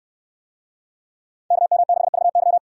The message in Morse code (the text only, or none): KM6VY